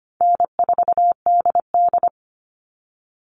Morse code, heard as a keyed tone, about 25 wpm, 700 Hz. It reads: N4DD
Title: Easy Call Signs Repeated 3x - 25wpm